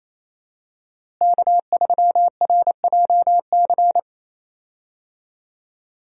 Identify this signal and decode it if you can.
Morse code, heard as a keyed tone, about 28 wpm, 700 Hz. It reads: K3RJC
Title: USA Call Signs - 5 Characters - Repeated 2x - 28wpm